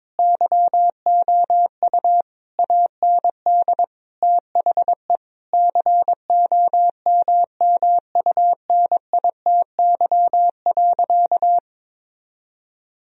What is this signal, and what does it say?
Morse code, heard as a keyed tone, about 21 wpm, 700 Hz. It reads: YOU AND THE COMMUNITY.
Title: Sentences from Top 1000 Words - Review Part 4 - 22wpm